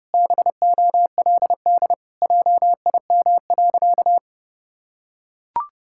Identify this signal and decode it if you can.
Morse code, heard as a keyed tone, about 30 wpm, 700 Hz. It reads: BOLD JIM.
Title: Pangram Challenge v2 - 30wpm